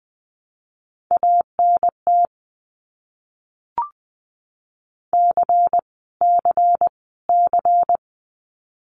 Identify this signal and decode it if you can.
Morse code, heard as a keyed tone, about 20 wpm, 700 Hz. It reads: ANT C C C
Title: Abbreviations Repeated 3x - 20wpm